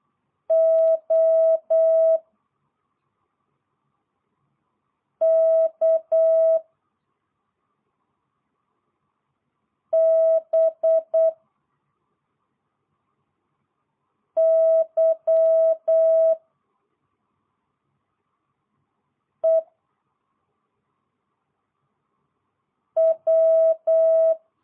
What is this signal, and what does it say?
Morse code, heard as a keyed tone, about 8 wpm, 650 Hz. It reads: O K B Y E W